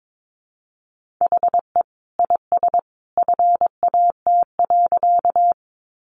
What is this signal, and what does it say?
Morse code, heard as a keyed tone, about 22 wpm, 700 Hz. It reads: HE IS FAT.
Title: Sentences from Top 600 Words 22wpm